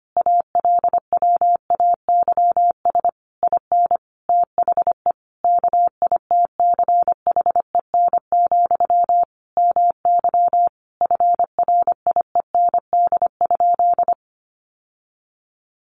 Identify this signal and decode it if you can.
Morse code, heard as a keyed tone, about 25 wpm, 700 Hz. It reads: ALWAYS IN THE KITCHEN, MY FRIEND?
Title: Sentences from Top 1400 Words 25wpm